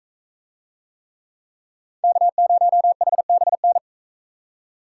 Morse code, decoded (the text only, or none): K0HBN